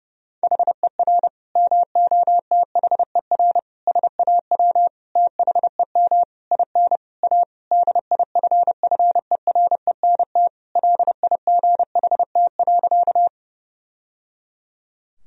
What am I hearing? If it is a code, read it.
Morse code, heard as a keyed tone, about 30 wpm, 700 Hz. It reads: HER MOTHER SAW THEM IN A DIFFERENT LIGHT.
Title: Sentences from Top 1000 Words - Review Part 3 - 30wpm